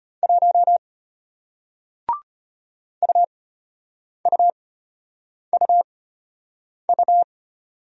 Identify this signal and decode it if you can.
Morse code, heard as a keyed tone, about 34 wpm, spaced as for 8 wpm, 700 Hz. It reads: 1 UUUU